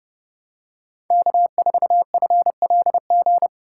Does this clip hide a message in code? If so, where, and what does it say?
Yes: Morse code from 1.1 to 3.5 s, heard as a keyed tone, about 28 wpm, 700 Hz. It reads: K4FLG